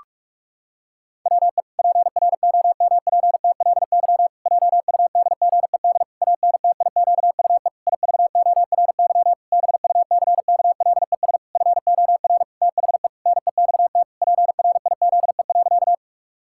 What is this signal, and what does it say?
Morse code, heard as a keyed tone, about 46 wpm, 700 Hz. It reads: WE PROMPTLY JUDGED ANTIQUE IVORY BUCKLES FOR THE NEXT PRIZE.
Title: Pangram Challenge v2 - 45wpm